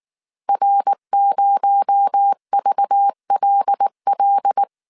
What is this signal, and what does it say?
Morse code, heard as a keyed tone, about 19 wpm, 800 Hz. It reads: R0VLL